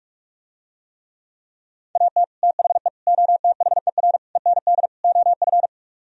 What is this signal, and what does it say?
Morse code, heard as a keyed tone, about 48 wpm, 700 Hz. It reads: AT THE OTHER END OF